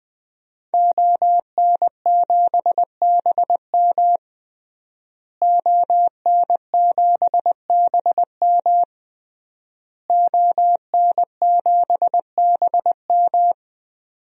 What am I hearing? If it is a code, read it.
Morse code, heard as a keyed tone, about 21 wpm, 700 Hz. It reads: ON7BM ON7BM ON7BM
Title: DXCC-100 Call Signs - 5 Characters - Repeated 3x - 20wpm